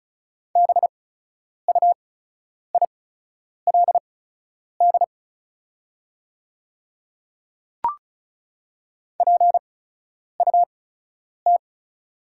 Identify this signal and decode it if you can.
Morse code, heard as a keyed tone, about 34 wpm, spaced as for 10 wpm, 690 Hz. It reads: BUILD PUT
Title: 8x Character Spacing - Top 300 Words - 35wpm